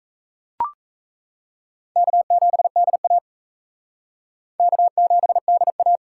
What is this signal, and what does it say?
Morse code, heard as a keyed tone, about 34 wpm, 700 Hz. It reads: K7DA K7DA
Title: Speed Racing - USA 1x2 Call Signs 28wpm